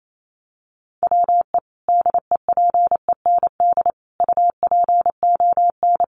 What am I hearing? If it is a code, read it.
Morse code, heard as a keyed tone, about 28 wpm, 700 Hz. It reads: WE DEPEND UPON
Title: Sentences from Top 1300 Words 28wpm